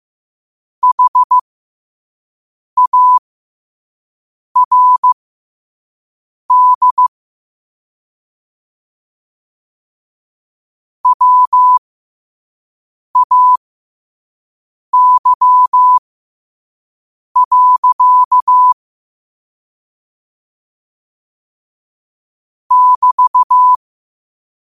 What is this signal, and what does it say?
Morse code, heard as a keyed tone, about 15 wpm, spaced as for 5 wpm, 1000 Hz. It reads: HARD WAY. <BT>
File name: qotdfiles / 5wpm